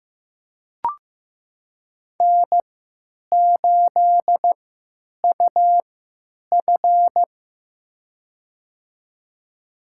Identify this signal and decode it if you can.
Morse code, heard as a keyed tone, about 15 wpm, spaced as for 9 wpm, 700 Hz. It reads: N8UF
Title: USA Call Signs - 4 Characters - 3x Character Spacing - 15wpm